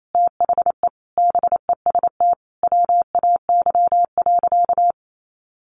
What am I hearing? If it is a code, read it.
Morse code, heard as a keyed tone, about 28 wpm, 700 Hz. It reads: THE BEST WAY.
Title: Sentences from Top 400 Words 28wpm